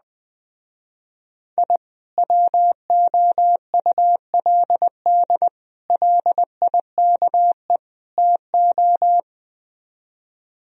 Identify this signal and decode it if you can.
Morse code, heard as a keyed tone, about 20 wpm, 700 Hz. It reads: I WOULD LIKE TO